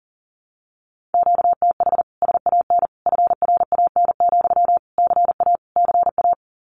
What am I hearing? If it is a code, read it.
Morse code, heard as a keyed tone, about 38 wpm, 700 Hz. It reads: QTH SAN FRAN, CA CA
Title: Common QSO Phrases Encore 40wpm